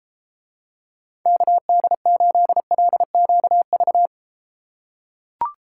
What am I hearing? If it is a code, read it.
Morse code, heard as a keyed tone, about 33 wpm, 700 Hz. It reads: KD8LQV